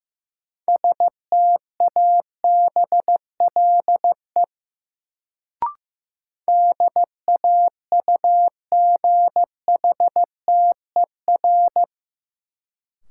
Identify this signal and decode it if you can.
Morse code, heard as a keyed tone, about 15 wpm, 700 Hz. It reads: STABLE DAUGHTER
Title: Sentences from Top 1500 Words - New Word Drill - 15wpm